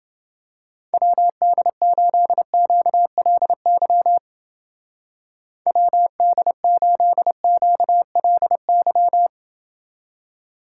Morse code, heard as a keyed tone, about 30 wpm, 700 Hz. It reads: WD8QLY WD8QLY